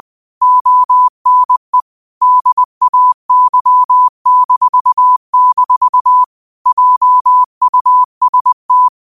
Morse code, heard as a keyed tone, about 20 wpm, 1000 Hz. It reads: ONE DAY-- JUST
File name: qotdfiles / 20wpm